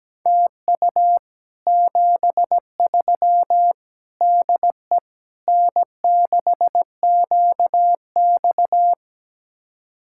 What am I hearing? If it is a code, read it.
Morse code, heard as a keyed tone, about 17 wpm, 700 Hz. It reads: TU 73 DE N6QX
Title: Speed-Racing Common QSO Phrases Encore - 17wpm